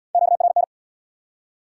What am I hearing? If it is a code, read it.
Morse code, heard as a keyed tone, about 69 wpm, 700 Hz. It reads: QRN